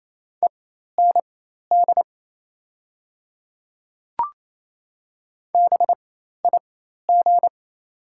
Morse code, heard as a keyed tone, about 28 wpm, spaced as for 13 wpm, 700 Hz. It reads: END BIG
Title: 4x Character Spacing - Top 300 Words - 28wpm